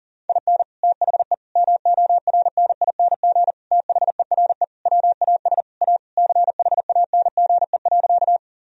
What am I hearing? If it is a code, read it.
Morse code, heard as a keyed tone, about 37 wpm, 700 Hz. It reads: IN THE MORNING THERE WAS A CHANGE.
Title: Sentences from Top 1000 Words - Review Part 1 - 40wpm